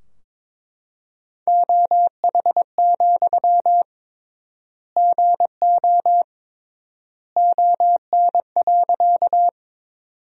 Morse code, heard as a keyed tone, about 21 wpm, 700 Hz. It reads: OH, GO ON.
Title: 3x Word-Spacing - Sentences from Top 500 Words - 22wpm